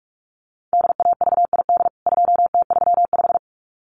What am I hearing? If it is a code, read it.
Morse code, heard as a keyed tone, about 44 wpm, 700 Hz. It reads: DAVID 2T35
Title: CWT Exchange 45wpm